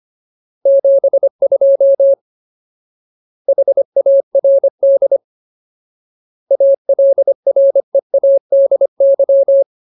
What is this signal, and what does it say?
Morse code, heard as a keyed tone, about 25 wpm, 550 Hz. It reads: 72 HARD ALREADY
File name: Ultimate Word List / 25WPM Ultimate